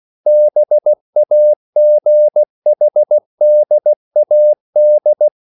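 Morse code, heard as a keyed tone, about 16 wpm, 600 Hz. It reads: BAGHDAD
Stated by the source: CW practice of national capitals at 16 words per minute, from ON6ZQ.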